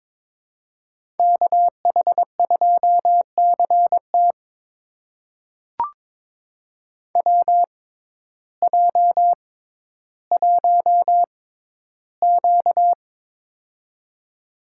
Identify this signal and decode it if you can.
Morse code, heard as a keyed tone, about 22 wpm, spaced as for 12 wpm, 700 Hz. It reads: KH2CT WJ1Q
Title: USA Call Signs - 6x Character Spacing - 22wpm